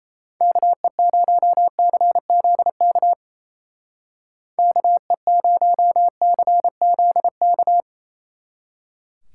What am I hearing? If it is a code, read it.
Morse code, heard as a keyed tone, about 37 wpm, 700 Hz. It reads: KE0CZK KE0CZK